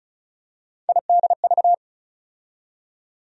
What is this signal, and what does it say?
Morse code, heard as a keyed tone, about 36 wpm, 700 Hz. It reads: IDV
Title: USA Call Sign Postfixes - 3 Letters 35wpm